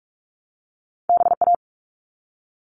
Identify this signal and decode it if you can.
Morse code, heard as a keyed tone, about 46 wpm, 700 Hz. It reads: BA
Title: All Cut Numbers - 2-Digit 45wpm